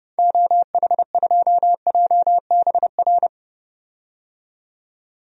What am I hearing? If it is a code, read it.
Morse code, heard as a keyed tone, about 30 wpm, 700 Hz. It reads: OH2JBR
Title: DXCC-100 Call Signs - 6 Characters - 30wpm